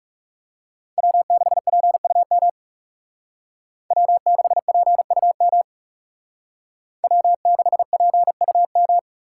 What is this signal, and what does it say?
Morse code, heard as a keyed tone, about 35 wpm, 700 Hz. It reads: W6PUM W6PUM W6PUM
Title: Speed Racing - USA 1x3 Call Signs 30wpm